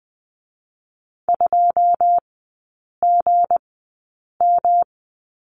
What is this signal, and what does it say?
Morse code, heard as a keyed tone, about 20 wpm, spaced as for 8 wpm, 700 Hz. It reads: 2GM
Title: 184 Lesson 30e - Letter b - Three Characters 20wpm